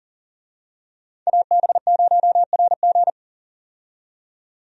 Morse code, heard as a keyed tone, about 40 wpm, 700 Hz. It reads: AB0RG
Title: USA Call Signs - 5 Characters - 40wpm